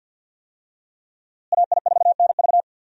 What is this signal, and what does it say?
Morse code, heard as a keyed tone, about 45 wpm, 700 Hz. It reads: AI4NV
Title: USA Call Signs - 5 Characters - Repeated 2x - 50wpm